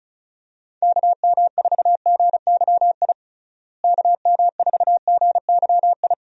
Morse code, heard as a keyed tone, about 34 wpm, 700 Hz. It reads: KM4GYI KM4GYI
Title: USA Call Signs - 6 Characters - Repeated 2x - 35wpm